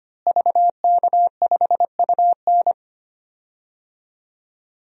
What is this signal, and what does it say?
Morse code, heard as a keyed tone, about 25 wpm, 700 Hz. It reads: VK5UN
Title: DXCC-100 Call Signs - 5 Characters - Repeated 3x - 25wpm